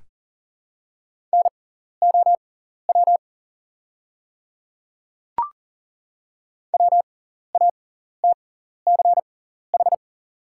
Morse code, heard as a keyed tone, about 40 wpm, spaced as for 14 wpm, 700 Hz. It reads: NOW WATCH
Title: 6x Character Spacing - Top 500 Words - 40wpm